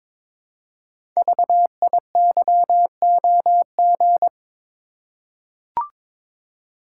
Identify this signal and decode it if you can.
Morse code, heard as a keyed tone, about 22 wpm, 700 Hz. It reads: VIYOG
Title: Code Groups - 5 Letters - 22wpm